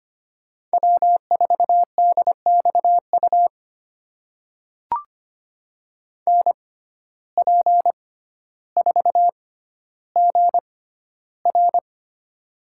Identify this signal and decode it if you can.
Morse code, heard as a keyed tone, about 25 wpm, spaced as for 13 wpm, 700 Hz. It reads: W4DXU NP4GR
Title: USA Call Signs - 5 Characters - 6x Character Spacing - 25wpm